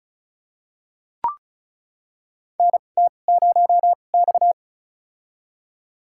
Morse code, heard as a keyed tone, about 34 wpm, spaced as for 25 wpm, 700 Hz. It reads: NT0X
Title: USA Call Signs - 4 Characters - 2x Character Spacing - 35wpm